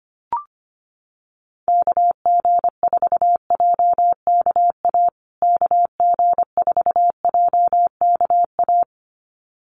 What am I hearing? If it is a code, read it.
Morse code, heard as a keyed tone, about 25 wpm, 700 Hz. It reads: KG4JKA KG4JKA
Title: USA Call Signs Repeated 2x 25wpm